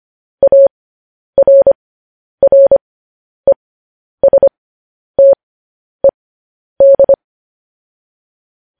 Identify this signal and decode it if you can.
Morse code, heard as a keyed tone, about 25 wpm, spaced as for 10 wpm, 550 Hz. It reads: ARRESTED
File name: podcast / News headlines-1x-25x10